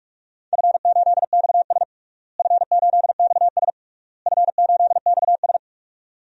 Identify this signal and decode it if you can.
Morse code, heard as a keyed tone, about 45 wpm, 700 Hz. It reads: F8XS F8XS F8XS
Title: DXCC-100 Call Signs - 4 Characters - Repeated 3x - 45wpm